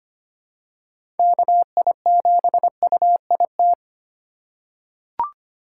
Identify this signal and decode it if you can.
Morse code, heard as a keyed tone, about 25 wpm, 700 Hz. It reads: KI7UIT
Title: USA Call Signs - 6 Characters - 4x Character Spacing - 25wpm